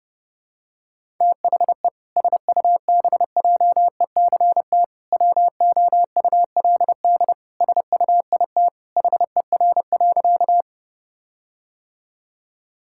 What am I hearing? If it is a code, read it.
Morse code, heard as a keyed tone, about 30 wpm, 700 Hz. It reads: THE SUBJECT WOULD SUIT HER.
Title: Sentences from Top 1600 Words 30wpm